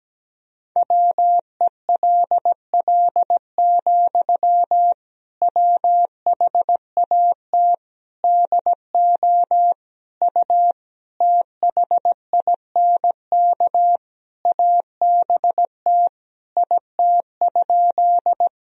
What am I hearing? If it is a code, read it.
Morse code, heard as a keyed tone, about 17 wpm, 700 Hz. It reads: WELL, WHAT DO U THINK ABT IT?
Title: Sentences from Top 200 Words and Abbreviations 17wpm